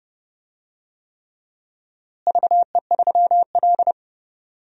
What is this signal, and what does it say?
Morse code, heard as a keyed tone, about 30 wpm, 700 Hz. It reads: VE3L
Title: DXCC-100 Call Signs - 4 Characters - Repeated 3x - 30wpm